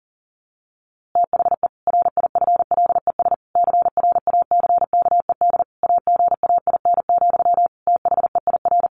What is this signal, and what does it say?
Morse code, heard as a keyed tone, about 40 wpm, 700 Hz. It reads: THE RIFLES CRACKED AGAIN, THEIR
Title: A Sound Of Thunder 40wpm